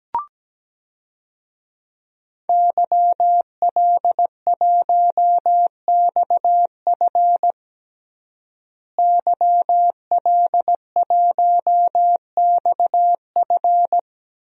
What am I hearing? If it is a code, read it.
Morse code, heard as a keyed tone, about 17 wpm, 700 Hz. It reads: YL1XF YL1XF
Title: DXCC-100 Call Signs - 5 Characters - Repeated 3x - 17wpm